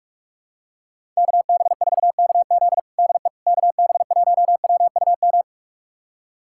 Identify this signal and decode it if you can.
Morse code, heard as a keyed tone, about 42 wpm, 700 Hz. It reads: KB4KZ DE KB1WUM
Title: USA Callsign DE Callsign 45wpm